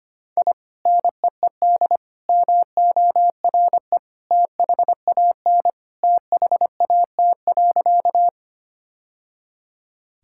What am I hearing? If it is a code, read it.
Morse code, heard as a keyed tone, about 25 wpm, 700 Hz. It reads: I NEED MORE THAN THAT.
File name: practice / 25wpm / Sentences with 2-Word Phrases - 25wpm